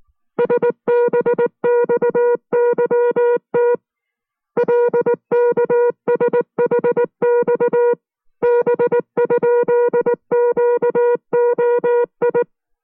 Morse code, heard as a keyed tone, about 19 wpm, 480 Hz. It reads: SBXYT LKSHX B?QOI